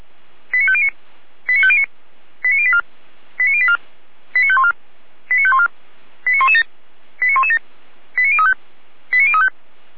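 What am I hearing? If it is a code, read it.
Five-tone selective call (ZVEI), heard as quick runs of five tones, tones of about 70 ms. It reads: 89509 89509 89084 89084 89514 89514 89117 89117 89035 89035
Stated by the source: ZVEI five-tone selective call received via SDR.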